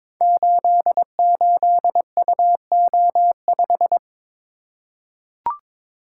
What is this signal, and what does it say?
Morse code, heard as a keyed tone, about 23 wpm, 690 Hz. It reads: 88UO5
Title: Code Groups - 5 Letter:Numbers - 22wpm